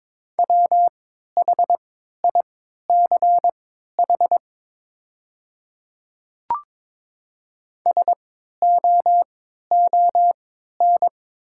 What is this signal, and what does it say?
Morse code, heard as a keyed tone, about 22 wpm, spaced as for 12 wpm, 700 Hz. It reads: WHICH SOON
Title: 3x Character Spacing - Top 500 Words - 22wpm